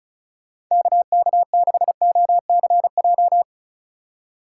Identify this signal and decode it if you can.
Morse code, heard as a keyed tone, about 43 wpm, 700 Hz. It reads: KK6OCJ
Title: Speed Racing - USA Call Signs 35wpm